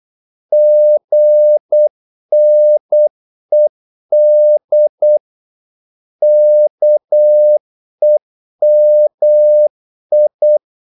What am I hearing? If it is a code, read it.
Morse code, heard as a keyed tone, about 8 wpm, 600 Hz. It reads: GNED KEMI
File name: Korte ord DK 8wpm